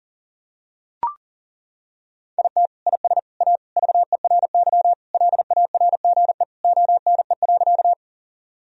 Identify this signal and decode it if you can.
Morse code, heard as a keyed tone, about 40 wpm, 700 Hz. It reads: IT IS A VERY LARGE ONE.
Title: Sentences from Top 1000 Words - Review Part 1 - 40wpm